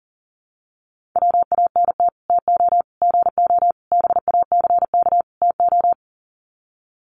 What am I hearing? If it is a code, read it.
Morse code, heard as a keyed tone, about 40 wpm, 700 Hz. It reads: WANT TO GO BACK TO
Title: Sets of 5 Words 40wpm